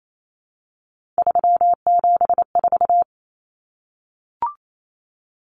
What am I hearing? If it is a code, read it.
Morse code, heard as a keyed tone, about 28 wpm, 700 Hz. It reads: 374
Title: Numbers - 3-Digit 28wpm